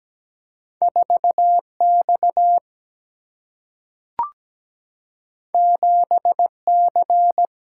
Morse code, heard as a keyed tone, about 17 wpm, 700 Hz. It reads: 4X 7C
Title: Single-Letter Single-Number 17wpm